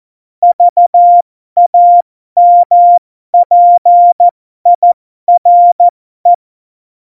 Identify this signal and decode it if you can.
Morse code, heard as a keyed tone, about 14 wpm, spaced as for 12 wpm, 700 Hz. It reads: VAMPIRE
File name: Morsekurs / Witze